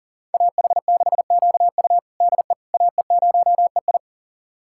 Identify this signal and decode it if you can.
Morse code, heard as a keyed tone, about 37 wpm, 700 Hz. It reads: AH6QU DE AE0EI
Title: USA Callsign DE Callsign - 5 Characters - 40wpm